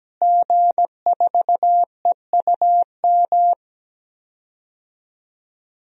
Morse code, heard as a keyed tone, about 17 wpm, 700 Hz. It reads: G4EUM
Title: DXCC-100 Call Signs - 5 Characters - Repeated 3x - 17wpm